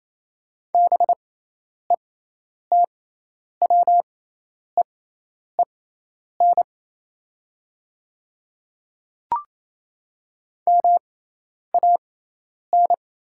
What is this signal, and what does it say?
Morse code, heard as a keyed tone, about 28 wpm, spaced as for 10 wpm, 700 Hz. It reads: BETWEEN MAN
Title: 6x Character Spacing - Top 500 Words - 28wpm